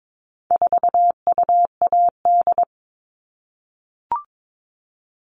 Morse code, heard as a keyed tone, about 22 wpm, 700 Hz. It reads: 4UAD
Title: All Cut Numbers - 4-Digit 22wpm